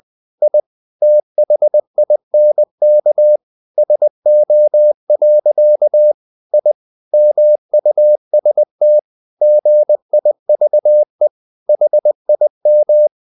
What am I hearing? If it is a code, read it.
Morse code, heard as a keyed tone, about 20 wpm, 600 Hz. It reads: I THINK SO. I MUST GIVE HIM
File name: the old man and the sea / the old man and the sea part 1 20wpm